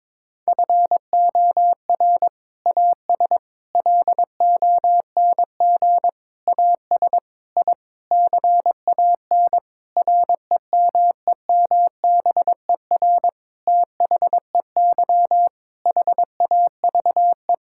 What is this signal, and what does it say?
Morse code, heard as a keyed tone, about 22 wpm, 700 Hz. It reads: FOR AS LONG AS I CAN REMEMBER THEY HAVE
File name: Sentences with 7-Word Phrases - 22wpm